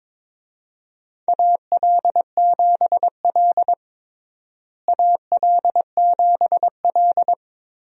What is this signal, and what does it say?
Morse code, heard as a keyed tone, about 22 wpm, 700 Hz. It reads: AL7L AL7L
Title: USA Call Signs - 4 Characters - Repeated 3x - 22wpm